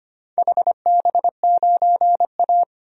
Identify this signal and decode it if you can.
Morse code, heard as a keyed tone, about 25 wpm, 700 Hz. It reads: HB9A
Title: DXCC-100 Call Signs - 4 Characters - Repeated 2x - 25wpm